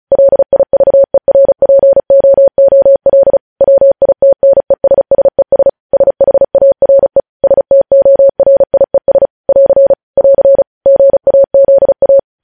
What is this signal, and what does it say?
Morse code, heard as a keyed tone, about 36 wpm, 550 Hz. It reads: LIVERPOOL WITNESSES SHARE STORIES <AR> <AR> GAZA
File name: podcast / News headlines-1x-35x35